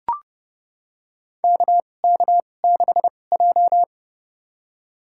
Morse code, heard as a keyed tone, about 30 wpm, spaced as for 22 wpm, 700 Hz. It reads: KK6J
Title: USA Call Signs - 4 Characters - 2x Character Spacing - 30wpm